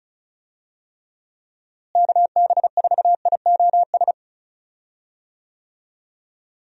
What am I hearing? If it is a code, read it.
Morse code, heard as a keyed tone, about 36 wpm, 700 Hz. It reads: KB4IOS
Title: USA Call Signs - 6 Characters - Repeated 2x - 35wpm